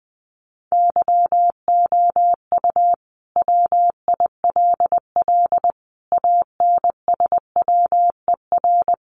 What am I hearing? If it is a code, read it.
Morse code, heard as a keyed tone, about 20 wpm, 700 Hz. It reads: YOU WILL ANSWER
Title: Sentences from Top 300 Words 20wpm